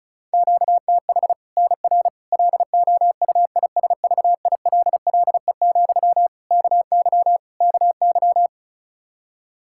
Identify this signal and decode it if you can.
Morse code, heard as a keyed tone, about 34 wpm, 700 Hz. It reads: QTH NR LOUISVILLE, KY KY
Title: K1USN SST QSO Practice 35wpm